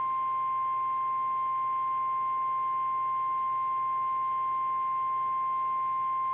Emergency Alert System (SAME) tones (the 1050 Hz attention tone). It is heard from 0.2 to 5.8 s.
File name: eas / KKPRKXPN / KKPRKXPN